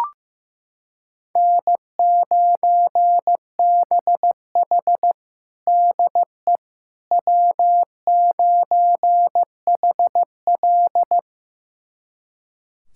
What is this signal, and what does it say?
Morse code, heard as a keyed tone, about 15 wpm, 700 Hz. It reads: N9BH DE W9HL
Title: USA Callsign DE Callsign - 4 Characters - 15wpm